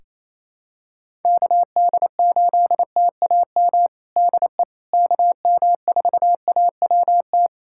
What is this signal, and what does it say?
Morse code, heard as a keyed tone, about 27 wpm, 700 Hz. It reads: KD8TAM DE KM4AWT
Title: USA Callsign DE Callsign - 6 Characters - 28wpm